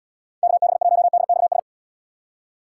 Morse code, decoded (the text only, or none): KB1DXS